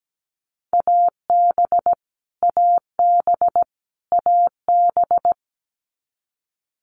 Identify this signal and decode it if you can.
Morse code, heard as a keyed tone, about 17 wpm, 700 Hz. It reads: AB AB AB
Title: Canadian Provinces and Territories - Repeated 3x - 17wpm